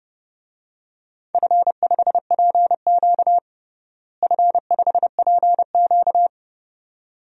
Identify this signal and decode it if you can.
Morse code, heard as a keyed tone, about 30 wpm, 700 Hz. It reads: F5PQ F5PQ
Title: DXCC-100 Call Signs - 4 Characters - Repeated 2x - 30wpm